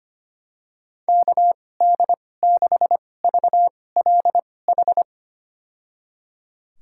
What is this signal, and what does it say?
Morse code, heard as a keyed tone, about 25 wpm, spaced as for 18 wpm, 700 Hz. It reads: KD6VLH
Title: USA Call Signs - 2x Character Spacing - 25wpm